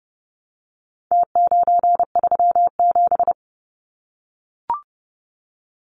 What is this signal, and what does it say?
Morse code, heard as a keyed tone, about 30 wpm, 700 Hz. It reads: T937
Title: Numbers - Contest Style - 4-Digit 30wpm